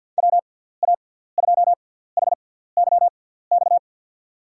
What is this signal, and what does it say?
Morse code, heard as a keyed tone, about 53 wpm, spaced as for 17 wpm, 700 Hz. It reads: WA2HYX
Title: USA Call Signs - 6 Characters - 6x Character Spacing - 50wpm